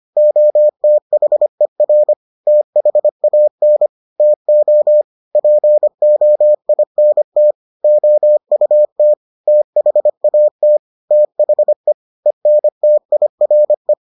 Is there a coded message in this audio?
Yes: Morse code from 0.2 to 13.9 s, heard as a keyed tone, about 25 wpm, 600 Hz. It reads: OTHER THAN TO POINT OUT THAT THE ENTIRE